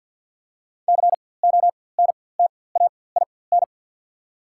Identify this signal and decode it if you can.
Morse code, heard as a keyed tone, about 54 wpm, spaced as for 24 wpm, 700 Hz. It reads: CONTAIN
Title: 4x Character Spacing - Top 500 Words - 50wpm